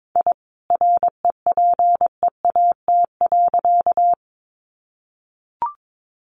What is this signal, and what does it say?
Morse code, heard as a keyed tone, about 22 wpm, 700 Hz. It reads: I REPEAT.
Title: A Sound Of Thunder 22wpm